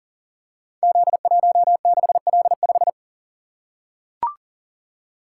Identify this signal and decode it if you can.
Morse code, heard as a keyed tone, about 38 wpm, 700 Hz. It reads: Z16L5
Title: Code Groups - 5 Letter:Numbers - 40wpm